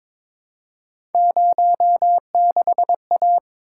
Morse code, heard as a keyed tone, about 22 wpm, 700 Hz. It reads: 06A
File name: practice / 22wpm / Three Letter-Number Combinations - 22wpm